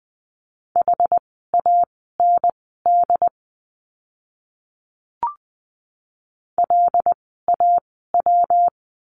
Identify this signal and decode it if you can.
Morse code, heard as a keyed tone, about 20 wpm, spaced as for 15 wpm, 700 Hz. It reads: HAND LAW
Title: 115 Lesson 19f - Letter w - Words 20wpm